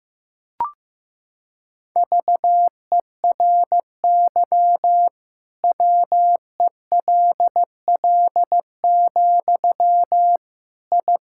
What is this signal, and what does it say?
Morse code, heard as a keyed tone, about 15 wpm, 700 Hz. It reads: VERY WELL, I
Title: Sentences from Top 1000 Words - Review Part 5 - 15wpm